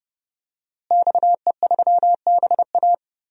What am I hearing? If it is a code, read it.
Morse code, heard as a keyed tone, about 30 wpm, 700 Hz. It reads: XE3BA
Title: DXCC-100 Call Signs - 5 Characters - Repeated 3x - 30wpm